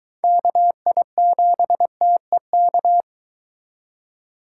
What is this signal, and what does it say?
Morse code, heard as a keyed tone, about 23 wpm, 700 Hz. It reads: KI7TEK